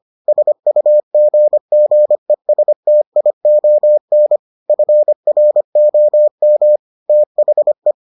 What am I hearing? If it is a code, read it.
Morse code, heard as a keyed tone, about 25 wpm, 600 Hz. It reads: SUGGESTION FROM THE